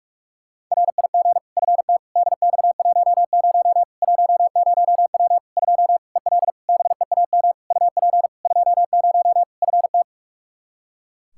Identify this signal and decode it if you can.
Morse code, heard as a keyed tone, about 42 wpm, 700 Hz. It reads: RIG FT DX10 10W 2 EL BEAM UP 20 FT
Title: Practice QSO - Exchange 2 - Info - 45wpm